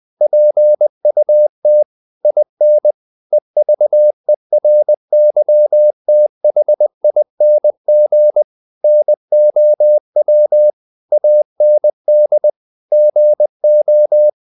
Morse code, heard as a keyed tone, about 20 wpm, 600 Hz. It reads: PUT IN EVERYTHING NOW AND GO
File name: the old man and the sea / the old man and the sea part 1 20wpm